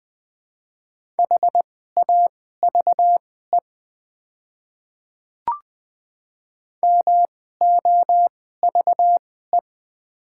Morse code, heard as a keyed tone, about 20 wpm, spaced as for 14 wpm, 700 Hz. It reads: HAVE MOVE
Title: 2x Character Spacing - Top 200 Words - 20wpm